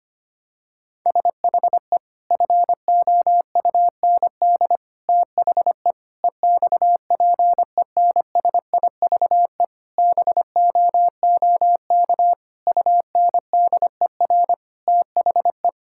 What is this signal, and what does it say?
Morse code, heard as a keyed tone, about 25 wpm, 700 Hz. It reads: SHE FOUND THE EXPENSIVE BOOK UNDER THE
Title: Sentences from Top 1000 Words - Review Part 6 - 25wpm